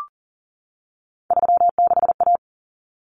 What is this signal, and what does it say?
Morse code, heard as a keyed tone, about 40 wpm, 700 Hz. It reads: 36A
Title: Common Cut Numbers - 3-Digit 40wpm